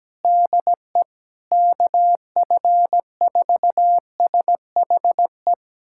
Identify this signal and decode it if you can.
Morse code, heard as a keyed tone, about 17 wpm, 700 Hz. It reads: DE KF4SHE
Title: USA Callsign DE Callsign 17wpm